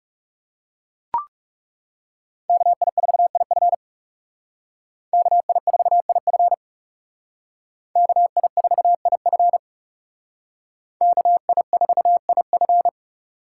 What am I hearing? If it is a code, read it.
Morse code, heard as a keyed tone, about 35 wpm, 700 Hz. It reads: KI4IF KI4IF KI4IF KI4IF
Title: Speed Racing - USA 2x2 Call Signs 30wpm